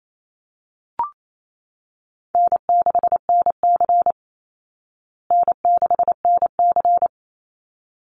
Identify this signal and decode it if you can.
Morse code, heard as a keyed tone, about 28 wpm, 700 Hz. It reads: N6NC N6NC
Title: Easy Call Signs Repeated 2x - 28wpm